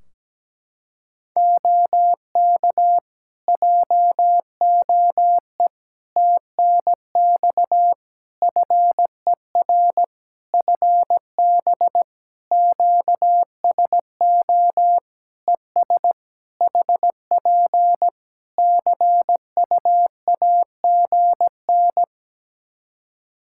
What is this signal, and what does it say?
Morse code, heard as a keyed tone, about 17 wpm, 700 Hz. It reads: OK JOE TNX FER FB QSO ES HP CUAGN
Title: Practice QSO - Exchange 4 - Ending - 17wpm